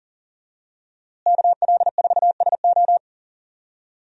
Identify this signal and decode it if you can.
Morse code, heard as a keyed tone, about 40 wpm, 700 Hz. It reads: KL4SO
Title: USA Call Signs - 5 Characters - Repeated 3x - 40wpm